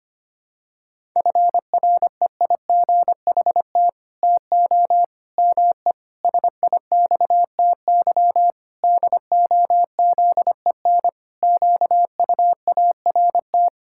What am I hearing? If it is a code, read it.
Morse code, heard as a keyed tone, about 25 wpm, 700 Hz. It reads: FREIGHT TO ME SIXTY DOZEN QUART
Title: Pangram Challenge v2 - 25wpm